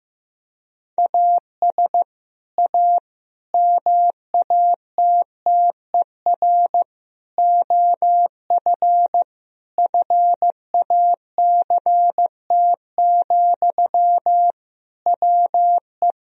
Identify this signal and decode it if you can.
Morse code, heard as a keyed tone, about 15 wpm, 700 Hz. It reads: AS A MATTER OF FACT, WE
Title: Sentences from Top 1000 Words - Review Part 3 - 15wpm